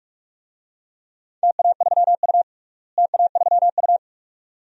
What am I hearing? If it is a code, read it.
Morse code, heard as a keyed tone, about 46 wpm, 700 Hz. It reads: TA3U TA3U
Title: DXCC-100 Call Signs - 4 Characters - Repeated 2x - 45wpm